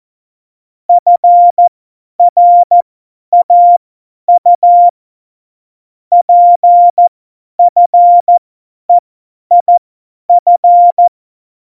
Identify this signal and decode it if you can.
Morse code, heard as a keyed tone, about 14 wpm, spaced as for 10 wpm, 700 Hz. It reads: FRAU PFEIF
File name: Morsekurs / Witze / Witz 29. September 2020, 50 F70